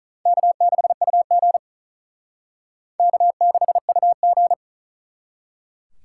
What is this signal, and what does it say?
Morse code, heard as a keyed tone, about 44 wpm, 700 Hz. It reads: K6UG K6UG
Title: Speed Racing - USA Call Signs 35wpm